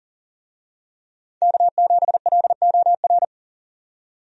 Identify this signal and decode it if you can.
Morse code, heard as a keyed tone, about 40 wpm, 700 Hz. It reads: K7LOR
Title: USA Call Signs - 5 Characters - 40wpm